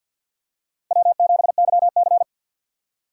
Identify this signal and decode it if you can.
Morse code, heard as a keyed tone, about 48 wpm, 700 Hz. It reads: W7YC
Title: USA Call Signs - 4 Characters - 6x Character Spacing - 50wpm